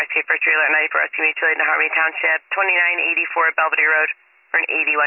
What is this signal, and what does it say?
Emergency Alert System (SAME) tones, heard as harsh buzzing bursts, with speech. It is heard from 0.3 to 4.1 s.